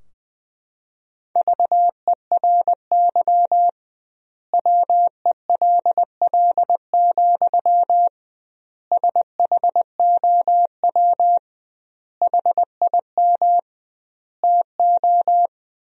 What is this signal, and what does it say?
Morse code, heard as a keyed tone, about 20 wpm, 700 Hz. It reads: VERY WELL, SHOW HIM TO
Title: 2x Word-Spacing - Sentences from Top 200 Words - 20wpm